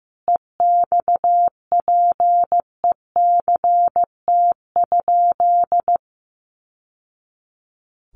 Morse code, heard as a keyed tone, about 15 wpm, 700 Hz. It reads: EXPECT?
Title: Sentences from Top 1100 Words 15wpm